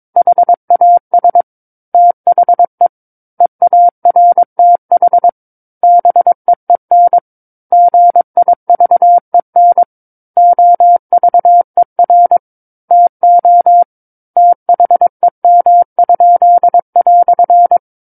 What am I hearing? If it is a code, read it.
Morse code, heard as a keyed tone, about 22 wpm, 700 Hz. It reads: HAS THE EARTH BEEN GIVEN OVER TO THEM?"
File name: Book One / 13-Chapter 13 - 22 WPM